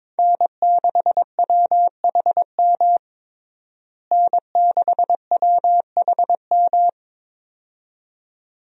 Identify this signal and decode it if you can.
Morse code, heard as a keyed tone, about 22 wpm, 700 Hz. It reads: N6WHM N6WHM
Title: USA Call Signs - 5 Characters - Repeated 2x - 22wpm